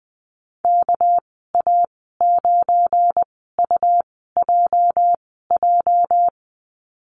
Morse code, heard as a keyed tone, about 20 wpm, spaced as for 15 wpm, 700 Hz. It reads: KA9UJJ
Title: 213 Lesson 34g - Number 8 - Call Signs 20wpm